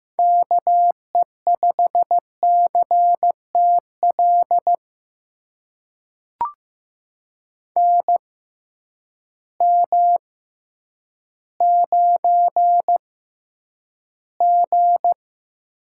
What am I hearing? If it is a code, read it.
Morse code, heard as a keyed tone, about 15 wpm, spaced as for 9 wpm, 700 Hz. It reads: KE5CTL NM9G
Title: USA Call Signs - 6x Character Spacing - 15wpm